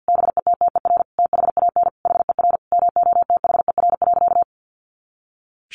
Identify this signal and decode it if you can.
Morse code, heard as a keyed tone, about 49 wpm, 700 Hz. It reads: BETTER THAN HER MOTHER.
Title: Sentences from Top 400 Words 50wpm